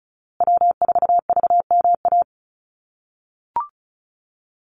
Morse code, heard as a keyed tone, about 34 wpm, 700 Hz. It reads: W4VMA
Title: USA Call Signs Repeated 2x 35wpm